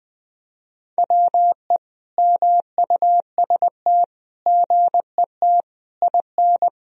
Morse code, heard as a keyed tone, about 21 wpm, 700 Hz. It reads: WE MUST GET IN
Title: Sentences from Top 1000 Words - Review Part 1 - 20wpm